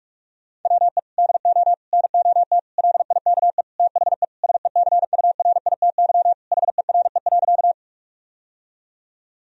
Morse code, heard as a keyed tone, about 46 wpm, 700 Hz. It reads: WE DO NOT LIKE THE SECURITY HERE.
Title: Sentences from Top 1000 Words - Review Part 4 - 45wpm